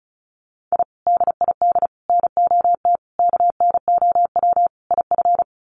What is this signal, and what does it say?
Morse code, heard as a keyed tone, about 34 wpm, 700 Hz. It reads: I DID NOT KNOW IF
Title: Sets of 5 Words Encore 35wpm